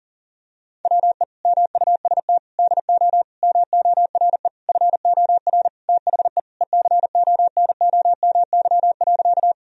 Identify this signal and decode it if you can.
Morse code, heard as a keyed tone, about 40 wpm, 700 Hz. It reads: WE MUST DO MORE FOR THE ECONOMY.
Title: Sentences from Top 1000 Words - Review Part 3 - 40wpm